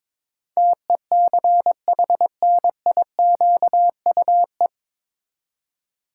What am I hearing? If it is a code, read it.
Morse code, heard as a keyed tone, about 21 wpm, 700 Hz. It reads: TECHNIQUE
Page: Morse Code Ninja practice 22wpm